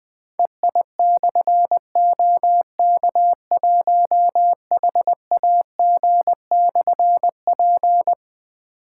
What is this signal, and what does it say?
Morse code, heard as a keyed tone, about 20 wpm, 700 Hz. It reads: EI/OK1HAG/P
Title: Difficult Call Signs 20wpm